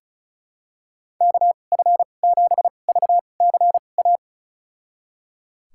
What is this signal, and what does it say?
Morse code, heard as a keyed tone, about 34 wpm, spaced as for 25 wpm, 700 Hz. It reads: KF7VCA
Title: USA Call Signs - 6 Characters - 2x Character Spacing - 35wpm